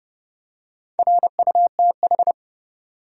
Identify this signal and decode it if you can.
Morse code, heard as a keyed tone, about 31 wpm, 700 Hz. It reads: RUTH
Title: USA Common Names - Max 4 Letters - 30wpm